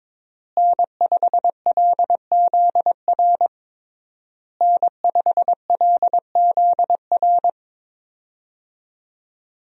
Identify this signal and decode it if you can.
Morse code, heard as a keyed tone, about 22 wpm, 700 Hz. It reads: N5LZR N5LZR
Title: USA Call Signs - 5 Characters - Repeated 3x - 22wpm